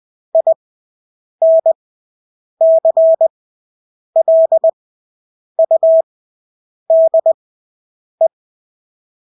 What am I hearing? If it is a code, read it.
Morse code, heard as a keyed tone, about 20 wpm, spaced as for 8 wpm, 650 Hz. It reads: INCLUDE
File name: Engelske ord 20/8 wpm